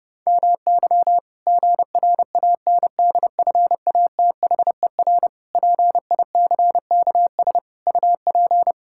Morse code, heard as a keyed tone, about 30 wpm, 700 Hz. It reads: MY GRANDFATHER PICKS UP
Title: Pangram Challenge v2 - 30wpm